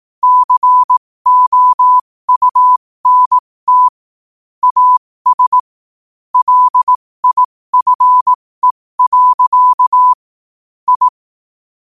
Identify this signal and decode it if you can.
Morse code, heard as a keyed tone, about 18 wpm, spaced as for 15 wpm, 1000 Hz. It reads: COUNT AS LIFE. I
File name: qotdfiles / 15wpm